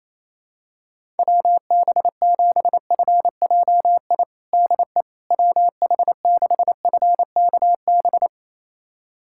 Morse code, heard as a keyed tone, about 27 wpm, 700 Hz. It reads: WB7FJI DE WH6FKB
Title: USA Callsign DE Callsign - 6 Characters - 28wpm